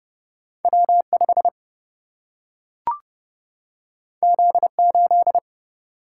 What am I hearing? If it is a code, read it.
Morse code, heard as a keyed tone, about 30 wpm, 700 Hz. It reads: W5 Z8
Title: Single-Letter Single-Number 30wpm